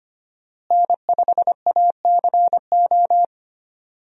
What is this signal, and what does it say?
Morse code, heard as a keyed tone, about 25 wpm, 700 Hz. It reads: N5ACO
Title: USA Call Signs - 5 Characters - Repeated 3x - 25wpm